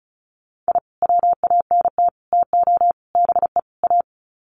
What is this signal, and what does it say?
Morse code, heard as a keyed tone, about 34 wpm, 700 Hz. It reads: I WANT TO BE A
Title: Sets of 5 Words Encore 35wpm